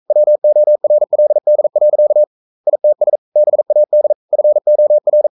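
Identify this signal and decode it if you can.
Morse code, heard as a keyed tone, about 42 wpm, 600 Hz. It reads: WORLD. ITS BAD FOR
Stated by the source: Morse code sending of an English novel at 42 wpm.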